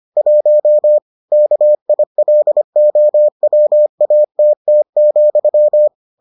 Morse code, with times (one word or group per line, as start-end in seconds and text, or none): 0.2-1.0 1
1.3-5.9 KILOWATT,